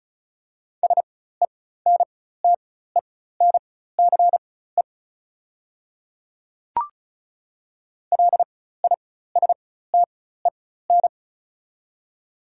Morse code, heard as a keyed tone, about 35 wpm, spaced as for 16 wpm, 690 Hz. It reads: SENTENCE LISTEN
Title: 4x Character Spacing - Top 500 Words - 35wpm